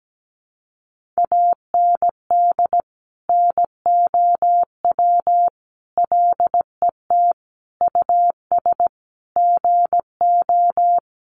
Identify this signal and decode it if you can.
Morse code, heard as a keyed tone, about 17 wpm, 700 Hz. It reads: AND NOW LET US GO
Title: Sentences from Top 300 Words 17wpm